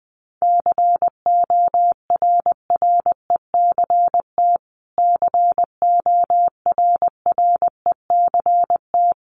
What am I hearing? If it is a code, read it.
Morse code, heard as a keyed tone, about 20 wpm, 700 Hz. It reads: CORRECT CORRECT
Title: Top 500 Words Repeated 20wpm